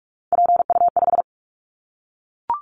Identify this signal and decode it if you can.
Morse code, heard as a keyed tone, about 44 wpm, 700 Hz. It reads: PU5
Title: Callsign Prefixes for Top 50 DXCC - 45wpm